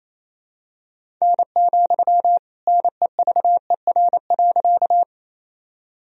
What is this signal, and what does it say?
Morse code, heard as a keyed tone, about 28 wpm, 700 Hz. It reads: N, NEVER.
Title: Sentences from Top 300 Words and Abbreviations 28wpm